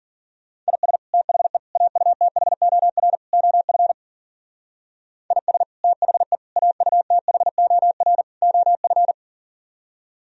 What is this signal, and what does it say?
Morse code, heard as a keyed tone, about 51 wpm, 700 Hz. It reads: IS THE AUTHOR OF IS THE AUTHOR OF